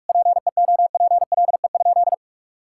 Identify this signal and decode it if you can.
Morse code, heard as a keyed tone, about 46 wpm, 700 Hz. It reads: PEOPLE?
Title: Sentences from Top 1000 Words - Review Part 5 - 45wpm